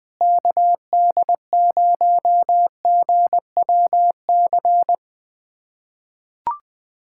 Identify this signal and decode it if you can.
Morse code, heard as a keyed tone, about 20 wpm, 700 Hz. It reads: KD0GWC
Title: USA Callsign DE Callsign - 6 Characters - 20wpm